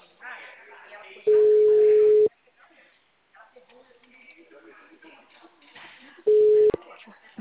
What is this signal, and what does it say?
Ringback tone (Europe, 425 Hz), with speech.
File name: out- -2002- - - .64254